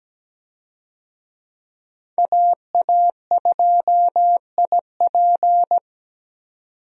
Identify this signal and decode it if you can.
Morse code, heard as a keyed tone, about 17 wpm, 700 Hz. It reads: AA2IP
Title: USA Call Signs - 5 Characters - 17wpm